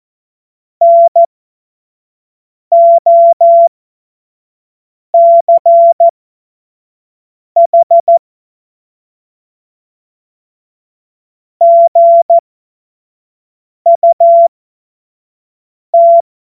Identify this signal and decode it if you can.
Morse code, heard as a keyed tone, about 14 wpm, spaced as for 5 wpm, 680 Hz. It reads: NOCH GUT
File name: Morsekurs / Witze / Witz 15. Februar 2022 25 F70